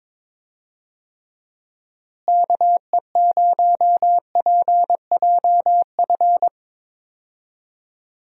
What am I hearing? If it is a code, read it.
Morse code, heard as a keyed tone, about 22 wpm, 700 Hz. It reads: KE0PJF
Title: USA Call Signs - 6 Characters - Repeated 3x - 22wpm